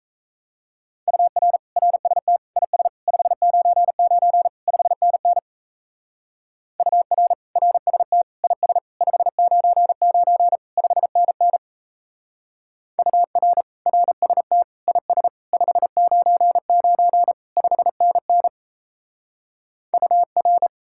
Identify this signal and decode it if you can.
Morse code, heard as a keyed tone, about 35 wpm, 700 Hz. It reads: UR RST IS 599 5NN UR RST IS 599 5NN UR RST IS 599 5NN UR
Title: Speed-Racing Common QSO Phrases Encore - 28wpm